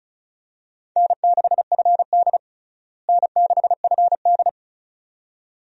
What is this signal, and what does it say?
Morse code, heard as a keyed tone, about 36 wpm, 700 Hz. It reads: N6FD N6FD
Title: USA Call Signs - 4 Characters - Repeated 2x - 35wpm